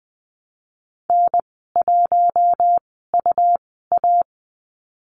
Morse code, heard as a keyed tone, about 19 wpm, spaced as for 14 wpm, 700 Hz. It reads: N1UA
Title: 200 Lesson 32g - Letter k - Call Signs 20wpm